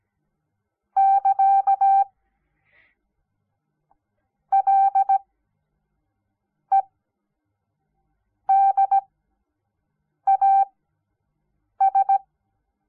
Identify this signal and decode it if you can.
Morse code, heard as a keyed tone, about 17 wpm, spaced as for 5 wpm, 780 Hz. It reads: <KA> LEDAS
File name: Onlinelehrgang / morsen / DJ4UF Morsen06